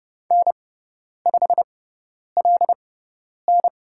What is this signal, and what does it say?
Morse code, heard as a keyed tone, about 30 wpm, spaced as for 10 wpm, 700 Hz. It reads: N5LN
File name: Lesson 18g - Letter m - Call Signs 30wpm